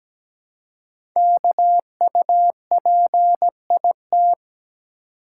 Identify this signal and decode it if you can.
Morse code, heard as a keyed tone, about 17 wpm, 700 Hz. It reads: KUPIT
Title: Code Groups - 5 Letter:Numbers - 17wpm